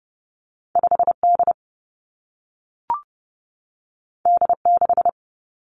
Morse code, heard as a keyed tone, about 30 wpm, 700 Hz. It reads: 5 D D 6
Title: Two Challenging Letter-Number Combinations 30wpm